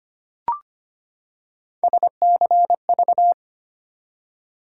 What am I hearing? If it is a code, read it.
Morse code, heard as a keyed tone, about 25 wpm, 700 Hz. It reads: SCV
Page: morse-code-ninja-practice-25wpm